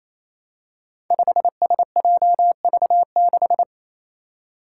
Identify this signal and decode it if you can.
Morse code, heard as a keyed tone, about 29 wpm, 700 Hz. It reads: 5SJV6
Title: Five Challenging Letter-Number Combinations 28wpm